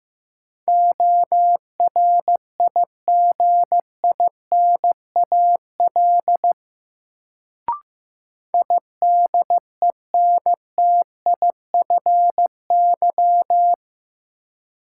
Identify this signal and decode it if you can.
Morse code, heard as a keyed tone, about 15 wpm, 700 Hz. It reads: ORIGINAL IDENTIFY
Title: Sentences from Top 900 Words - New Word Drill - 15wpm